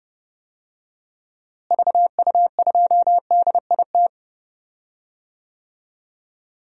Morse code, heard as a keyed tone, about 30 wpm, 700 Hz. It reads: VU2DIT
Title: DXCC-100 Call Signs - 6 Characters - Repeated 3x - 30wpm